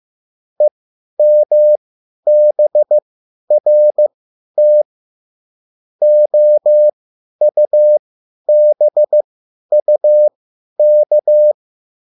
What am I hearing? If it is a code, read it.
Morse code, heard as a keyed tone, about 15 wpm, spaced as for 10 wpm, 600 Hz. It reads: EMBRT OUBUK
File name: QSL / CW lesson